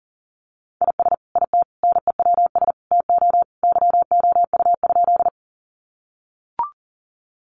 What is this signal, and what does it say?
Morse code, heard as a keyed tone, about 40 wpm, 700 Hz. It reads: IS IT NEWS TO YOU?
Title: Sentences from Top 800 Words 40wpm